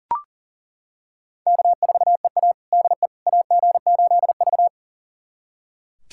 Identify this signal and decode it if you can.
Morse code, heard as a keyed tone, about 40 wpm, 700 Hz. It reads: K4EA DE AG8V
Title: USA Callsign DE Callsign - 4 Characters - 40wpm